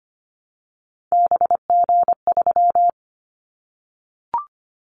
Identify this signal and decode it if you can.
Morse code, heard as a keyed tone, about 25 wpm, 700 Hz. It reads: BG3
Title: Callsign Prefixes for Top 100 DXCC - 25wpm